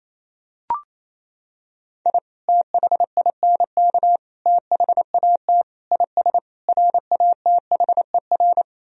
Morse code, heard as a keyed tone, about 27 wpm, 700 Hz. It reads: I THINK THAT IS RATHER
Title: Sentences from Top 1000 Words - Review Part 4 - 28wpm